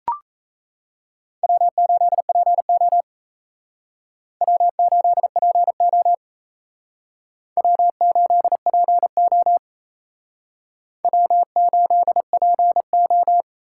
Morse code, heard as a keyed tone, about 36 wpm, 700 Hz. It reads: W8PO W8PO W8PO W8PO